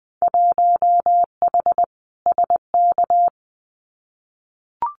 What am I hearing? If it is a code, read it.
Morse code, heard as a keyed tone, about 20 wpm, 700 Hz. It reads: 1H SK
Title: Winter Field Day Exchanges 20wpm